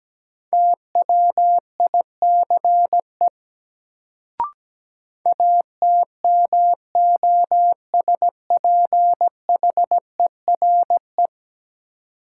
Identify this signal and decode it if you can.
Morse code, heard as a keyed tone, about 17 wpm, 700 Hz. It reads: TWICE ATMOSPHERE
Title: Sentences from Top 1600 Words - New Word Drill - 17wpm